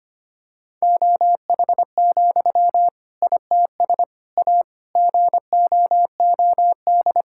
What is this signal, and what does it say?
Morse code, heard as a keyed tone, about 25 wpm, 700 Hz. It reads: OH, ITS A GOOD
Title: Sentences from Top 1000 Words - Review Part 5 - 25wpm